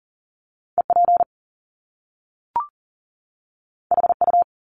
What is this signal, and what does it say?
Morse code, heard as a keyed tone, about 41 wpm, 700 Hz. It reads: EP HU
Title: Top Two Letter Combinations in Words 40wpm